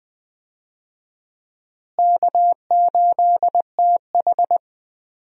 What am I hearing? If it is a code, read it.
Morse code, heard as a keyed tone, about 21 wpm, 700 Hz. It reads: K8TH
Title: USA Call Signs - 4 Characters - Repeated 3x - 20wpm